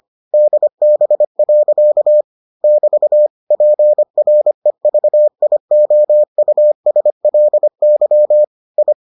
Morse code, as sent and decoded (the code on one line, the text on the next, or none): -.. -... .-.-.- / -...- / .--. .-. ...- .. --- ..- ... .-.. -.-- / ..
DB. <BT> PREVIOUSLY I